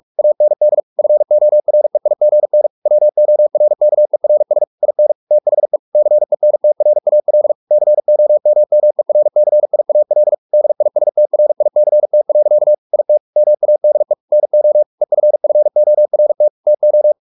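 Morse code, heard as a keyed tone, about 44 wpm, 600 Hz. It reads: AND FOREIGN WORKERS IN THE CENTRAL COMMERCIAL DISTRICT. IT MADE NO EFFORT TO
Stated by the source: Morse practice at 45 wpm, the novel Never by Ken Follett in English, from CW med Gnister.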